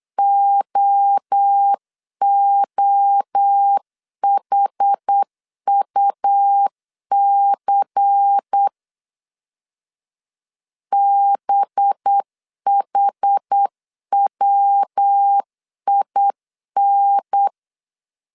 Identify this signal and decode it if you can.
Morse code, heard as a keyed tone, about 8 wpm, 800 Hz. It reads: OOHUC BHWIN
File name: CW / DET / 1 6 1500C S C 8wpm 5219